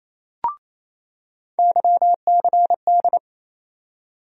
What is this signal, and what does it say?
Morse code, heard as a keyed tone, about 28 wpm, 700 Hz. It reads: YCD
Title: USA Call Sign Postfixes - 3 Letters 28wpm